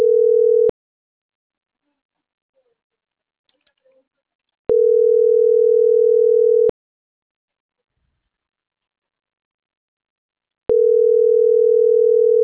Ringback tone (North America), 440 + 480 Hz.